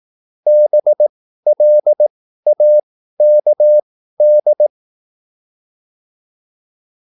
Morse code, heard as a keyed tone, about 18 wpm, spaced as for 13 wpm, 600 Hz. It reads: BLAKD